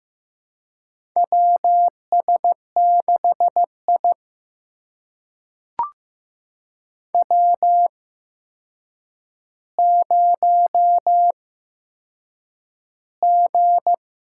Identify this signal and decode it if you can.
Morse code, heard as a keyed tone, about 15 wpm, spaced as for 7 wpm, 700 Hz. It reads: WS6I W0G
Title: USA Call Signs - 4 Characters - 8x Character Spacing - 15wpm